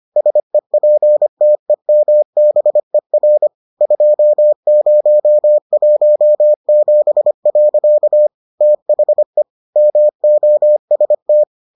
Morse code, heard as a keyed tone, about 25 wpm, 600 Hz. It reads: SEPTEMBER 2017. THE MOST